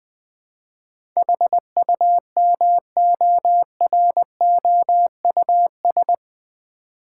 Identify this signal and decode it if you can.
Morse code, heard as a keyed tone, about 20 wpm, 700 Hz. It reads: HUMOROUS
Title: Eight Letter Words - 20wpm